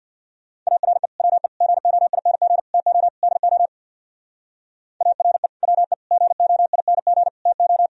Morse code, heard as a keyed tone, about 64 wpm, 700 Hz. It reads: ARE WE GOING TO DO ARE WE GOING TO